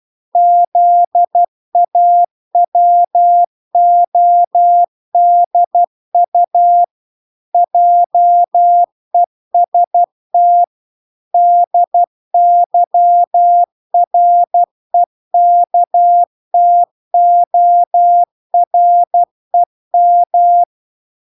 Morse code, heard as a keyed tone, about 12 wpm, 700 Hz. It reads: ZAWODU JEST DYREKTOREM